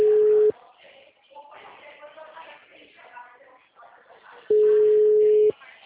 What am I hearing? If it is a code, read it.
Ringback tone (Europe, 425 Hz), with speech.